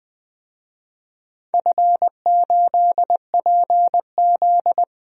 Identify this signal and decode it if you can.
Morse code, heard as a keyed tone, about 20 wpm, 700 Hz. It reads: F8PZ
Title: DXCC-100 Call Signs - 4 Characters - 20wpm